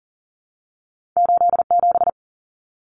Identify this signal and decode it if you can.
Morse code, heard as a keyed tone, about 42 wpm, 700 Hz. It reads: 8 7
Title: Common Cut Numbers - 2-Digit 40wpm